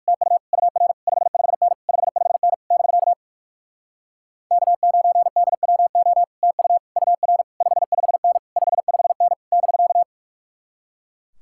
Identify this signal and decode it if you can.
Morse code, heard as a keyed tone, about 69 wpm, spaced as for 58 wpm, 700 Hz. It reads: TU UR 55N 55N <BK> K9DWO TU UR 55N 55N <BK>